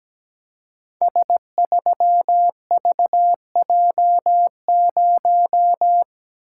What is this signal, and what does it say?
Morse code, heard as a keyed tone, about 17 wpm, 700 Hz. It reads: S3VJ0
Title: Code Groups - 5 Letter:Numbers - 17wpm